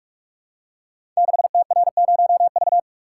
Morse code, heard as a keyed tone, about 44 wpm, 700 Hz. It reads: 6TR0V
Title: Code Groups - 5 Letter:Numbers - 45wpm